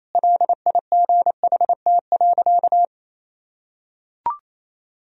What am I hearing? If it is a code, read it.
Morse code, heard as a keyed tone, about 28 wpm, 700 Hz. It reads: LIGHT.
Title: Sentences from Top 1000 Words - Review Part 2 - 28wpm